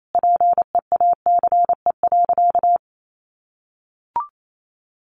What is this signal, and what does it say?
Morse code, heard as a keyed tone, about 28 wpm, 700 Hz. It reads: PEACE.
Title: Sentences from Top 1500 Words 28wpm